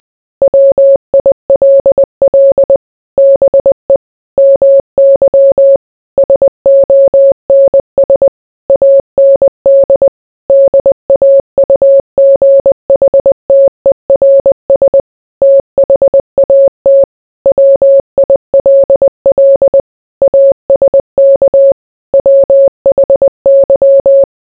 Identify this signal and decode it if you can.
Morse code, heard as a keyed tone, about 20 wpm, 550 Hz. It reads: WILL BE MY SONS AND DAUGHTERS THAT WILL ASK WHY